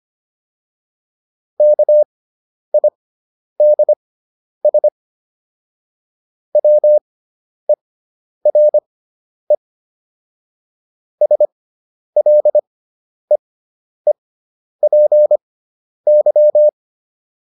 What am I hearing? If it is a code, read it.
Morse code, heard as a keyed tone, about 25 wpm, spaced as for 10 wpm, 600 Hz. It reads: KIDS WERE SLEEPY